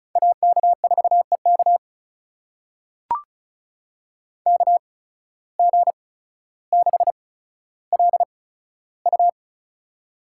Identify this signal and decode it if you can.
Morse code, heard as a keyed tone, about 36 wpm, spaced as for 15 wpm, 700 Hz. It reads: AK4EK KG6LU
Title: USA Call Signs - 5 Characters - 8x Character Spacing - 35wpm